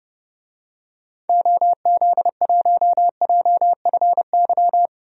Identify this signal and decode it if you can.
Morse code, heard as a keyed tone, about 31 wpm, 700 Hz. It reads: OZ1JFY
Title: DXCC-100 Call Signs - 6 Characters - 30wpm